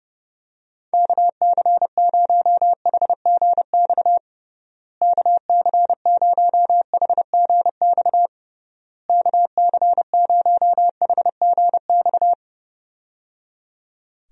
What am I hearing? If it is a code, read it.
Morse code, heard as a keyed tone, about 30 wpm, 700 Hz. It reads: KC0HGX KC0HGX KC0HGX
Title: USA Call Signs - 6 Characters - Repeated 3x - 30wpm